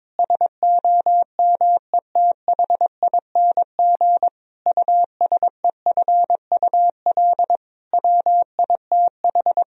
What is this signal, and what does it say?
Morse code, heard as a keyed tone, about 22 wpm, 700 Hz. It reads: SOMETHING USEFUL WITH
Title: Sentences from Top 1000 Words - Review Part 4 - 22wpm